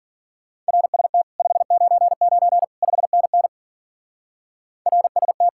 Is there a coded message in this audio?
Yes: Morse code from 0.7 to 5.5 s, heard as a keyed tone, about 50 wpm, 700 Hz. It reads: RST 599 5NN RST